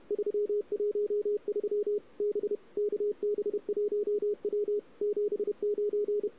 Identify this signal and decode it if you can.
Morse code, heard as a keyed tone, about 32 wpm, 410 Hz. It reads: 313 B KB1W 79
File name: CW / 2008-ss-cw-k5tr-op-n5rz -03:00